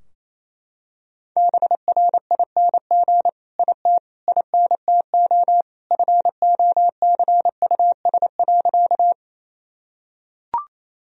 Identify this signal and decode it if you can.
Morse code, heard as a keyed tone, about 28 wpm, 700 Hz. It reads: BRING IT INTO FOCUS.
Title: Sentences from Top 1000 Words - Review Part 4 - 28wpm